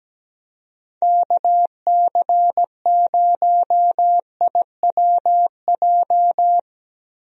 Morse code, heard as a keyed tone, about 17 wpm, 700 Hz. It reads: KC0IWJ
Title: USA Call Signs - 6 Characters - Repeated 3x - 17wpm